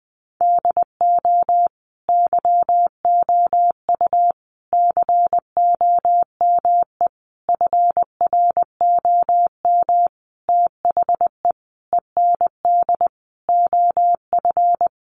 Morse code, heard as a keyed tone, about 20 wpm, 700 Hz. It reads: DO YOU COME FROM THE END OF
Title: Sentences from Top 200 Words 20wpm